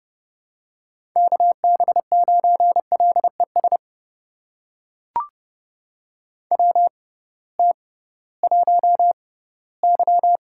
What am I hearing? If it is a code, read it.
Morse code, heard as a keyed tone, about 30 wpm, spaced as for 18 wpm, 700 Hz. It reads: KB9LES WT1Y
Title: USA Call Signs - 6x Character Spacing - 30wpm